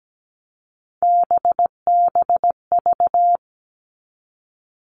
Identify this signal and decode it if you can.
Morse code, heard as a keyed tone, about 17 wpm, 700 Hz. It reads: BBV
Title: All Cut Numbers - 3-Digit 17wpm